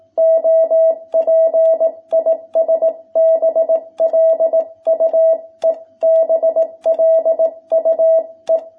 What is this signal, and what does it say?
Morse code, heard as a keyed tone, about 18 wpm, 660 Hz. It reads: OP IS BLUE BLUE
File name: jh8cbh / 2021cw